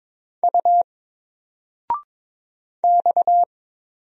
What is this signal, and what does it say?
Morse code, heard as a keyed tone, about 22 wpm, 700 Hz. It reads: U X
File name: Single-Letter-Number-22wpm